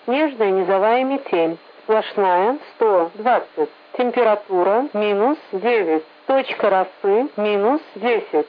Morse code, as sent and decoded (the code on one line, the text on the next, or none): none